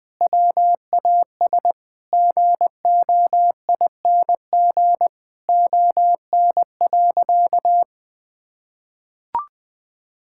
Morse code, heard as a keyed tone, about 20 wpm, 700 Hz. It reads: WAS GOING ON.
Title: Sentences with 7-Word Phrases - 20wpm